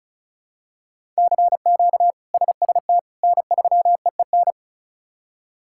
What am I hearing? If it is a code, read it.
Morse code, heard as a keyed tone, about 34 wpm, 700 Hz. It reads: CQ SST N3EEN
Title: K1USN SST QSO Practice - 35wpm